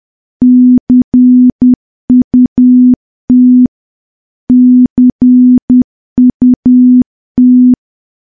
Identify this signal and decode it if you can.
Morse code, heard as a keyed tone, about 10 wpm, 250 Hz. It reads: CUT CUT
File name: morse / morse-training-10WPM